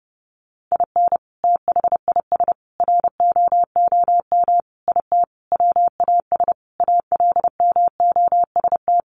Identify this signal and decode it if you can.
Morse code, heard as a keyed tone, about 30 wpm, 700 Hz. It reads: IN THIS ROOM IT WAS ALMOST
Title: Sentences from Top 700 Words 30wpm